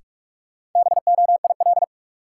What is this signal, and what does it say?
Morse code, heard as a keyed tone, about 44 wpm, 700 Hz. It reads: BOIL
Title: Sets of 5 Words Encore - New Word Drill - 45wpm